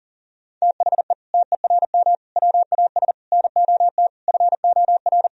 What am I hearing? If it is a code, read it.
Morse code, heard as a keyed tone, about 40 wpm, 700 Hz. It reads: THE TERM WAS NOT FOR
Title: Sentences from Top 1000 Words - Review Part 4 - 40wpm